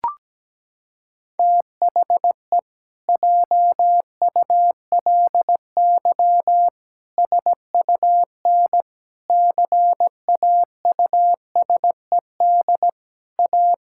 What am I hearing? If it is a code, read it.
Morse code, heard as a keyed tone, about 17 wpm, 700 Hz. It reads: THE JULY SUN CAUSED A
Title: Pangram Challenge v2 - 17wpm